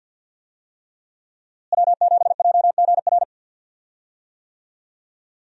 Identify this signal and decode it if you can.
Morse code, heard as a keyed tone, about 51 wpm, 700 Hz. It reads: W7JGR
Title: USA Call Signs - 5 Characters - Repeated 3x - 50wpm